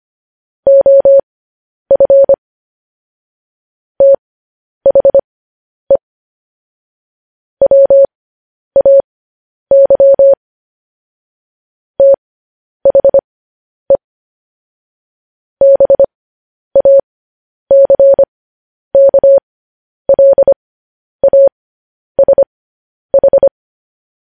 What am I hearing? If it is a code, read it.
Morse code, heard as a keyed tone, about 25 wpm, spaced as for 10 wpm, 550 Hz. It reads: OF THE WAY THE BACKLASH
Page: news-headlines-in-morse-code-at-10-wpm